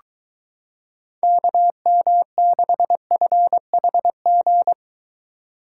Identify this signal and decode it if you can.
Morse code, heard as a keyed tone, about 22 wpm, 700 Hz. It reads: KM6FHG